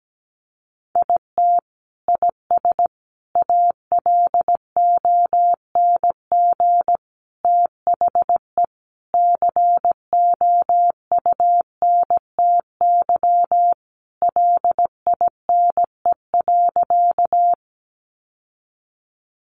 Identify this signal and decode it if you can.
Morse code, heard as a keyed tone, about 17 wpm, 700 Hz. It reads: IT IS ALONG THE COUNTY LINE.
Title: Sentences from Top 1200 Words 17wpm